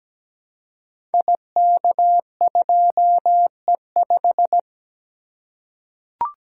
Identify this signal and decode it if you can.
Morse code, heard as a keyed tone, about 17 wpm, 700 Hz. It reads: IK2E5
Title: Code Groups - 5 Letter:Numbers - 17wpm